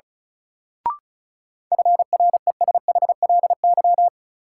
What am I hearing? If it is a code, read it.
Morse code, heard as a keyed tone, about 36 wpm, 700 Hz. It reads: FRESHLY
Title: Sets of 5 Words Encore - New Word Drill - 35wpm